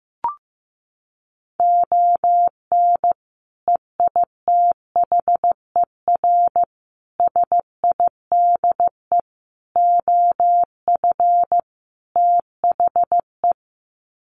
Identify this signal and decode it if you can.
Morse code, heard as a keyed tone, about 15 wpm, 700 Hz. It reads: ON EITHER SIDE OF THE
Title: Sets of 5 Words 15wpm